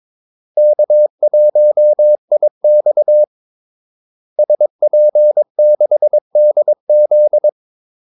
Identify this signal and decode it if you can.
Morse code, heard as a keyed tone, about 22 wpm, 600 Hz. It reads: K1IX SP6DZ
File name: Callsigns-22wpm